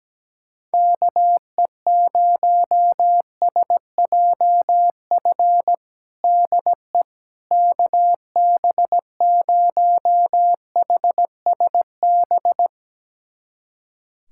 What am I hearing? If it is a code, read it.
Morse code, heard as a keyed tone, about 17 wpm, 700 Hz. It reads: KE0SJF DE KB0HSB
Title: USA Callsign DE Callsign - 6 Characters - 17wpm